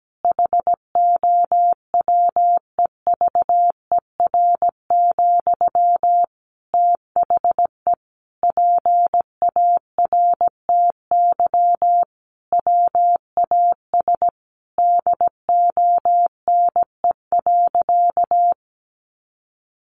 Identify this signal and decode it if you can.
Morse code, heard as a keyed tone, about 17 wpm, 700 Hz. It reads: HOWEVER, THE PARTY WAS DONE.
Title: Sentences from Top 1400 Words 17wpm